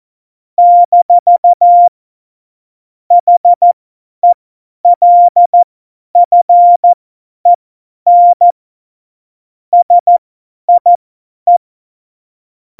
Morse code, heard as a keyed tone, about 14 wpm, spaced as for 10 wpm, 700 Hz. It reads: - HELFEN SIE
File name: Morsekurs / Witze / Witz 27 March 2018, 50 F70